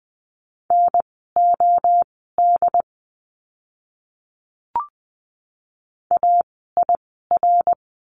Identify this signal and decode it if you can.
Morse code, heard as a keyed tone, about 20 wpm, spaced as for 14 wpm, 700 Hz. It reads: NOD AIR
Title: 068 Lesson 12f - Letter d - Words 20wpm